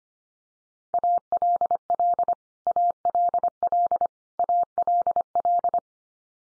Morse code, heard as a keyed tone, about 25 wpm, 700 Hz. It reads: ALL ALL ALL
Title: Top 100 Words 25wpm x3